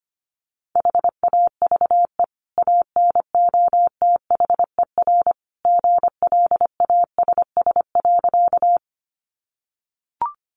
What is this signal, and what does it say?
Morse code, heard as a keyed tone, about 25 wpm, 700 Hz. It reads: HAVE ANOTHER GLASS.
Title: Sentences from Top 1300 Words 25wpm